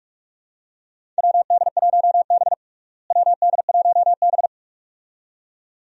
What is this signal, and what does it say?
Morse code, heard as a keyed tone, about 46 wpm, 700 Hz. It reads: WD1B WD1B
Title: USA Call Signs - 4 Characters - Repeated 2x - 45wpm